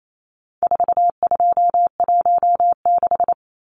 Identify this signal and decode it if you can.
Morse code, heard as a keyed tone, about 28 wpm, 700 Hz. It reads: 4216
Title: Numbers - 4-Digit 28wpm